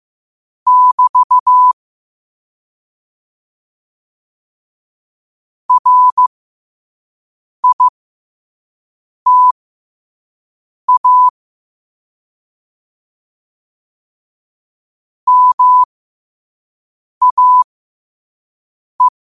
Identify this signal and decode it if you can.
Morse code, heard as a keyed tone, about 15 wpm, spaced as for 5 wpm, 1000 Hz. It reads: <BT> RITA MAE
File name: qotdfiles / 5wpm